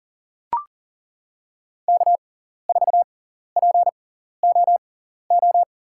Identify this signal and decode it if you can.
Morse code, heard as a keyed tone, about 40 wpm, spaced as for 14 wpm, 700 Hz. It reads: K4POO
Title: USA Call Signs - 5 Characters - 6x Character Spacing - 40wpm